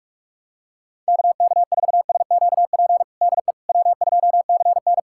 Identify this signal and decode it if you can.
Morse code, heard as a keyed tone, about 46 wpm, 700 Hz. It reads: KK4SQP DE W2CN
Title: USA Callsign DE Callsign 45wpm